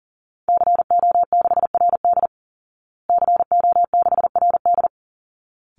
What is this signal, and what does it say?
Morse code, heard as a keyed tone, about 37 wpm, 700 Hz. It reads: CO6RD CO6RD
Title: Call Signs Repeated 3x - 40wpm